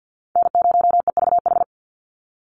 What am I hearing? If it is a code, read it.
Morse code, heard as a keyed tone, about 50 wpm, 700 Hz. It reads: N0EVH
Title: NA SOTA Chaser Callsigns 50wpm